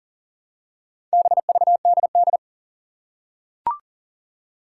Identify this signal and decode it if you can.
Morse code, heard as a keyed tone, about 39 wpm, 700 Hz. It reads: BVDD
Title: Four Challenging Letter-Number Combinations 40wpm